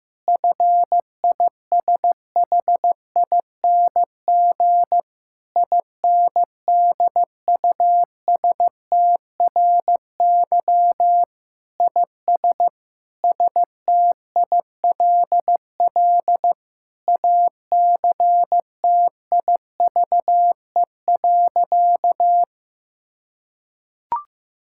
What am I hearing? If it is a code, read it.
Morse code, heard as a keyed tone, about 15 wpm, 700 Hz. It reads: FISHING INDUSTRY IS STILL ACTIVE.
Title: Sentences from Top 1000 Words - Review Part 6 - 15wpm